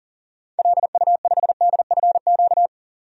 Morse code, heard as a keyed tone, about 40 wpm, 700 Hz. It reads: LU5DFQ
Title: DXCC-100 Call Signs - 6 Characters - 40wpm